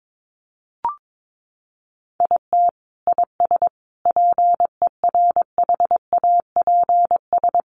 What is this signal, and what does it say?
Morse code, heard as a keyed tone, about 22 wpm, 700 Hz. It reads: IT IS PERHAPS
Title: Sentences from Top 1300 Words 22wpm